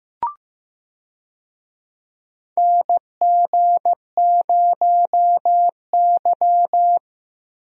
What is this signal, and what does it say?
Morse code, heard as a keyed tone, about 15 wpm, 700 Hz. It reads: NG0Y
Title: USA Call Signs - 4 Characters - Repeated 2x - 15wpm